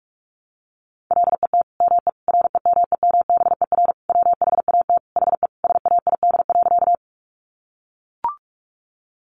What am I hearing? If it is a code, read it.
Morse code, heard as a keyed tone, about 44 wpm, 700 Hz. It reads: LET ME REMEMBER WHAT HE SAID.
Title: Sentences from Top 400 Words 45wpm